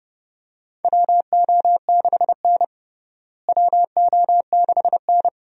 Morse code, heard as a keyed tone, about 31 wpm, 700 Hz. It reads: WO6N WO6N
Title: USA Call Signs - 4 Characters - Repeated 2x - 30wpm